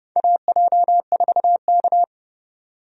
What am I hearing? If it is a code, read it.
Morse code, heard as a keyed tone, about 30 wpm, 700 Hz. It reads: AJ4K
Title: USA Call Signs - 4 Characters - 30wpm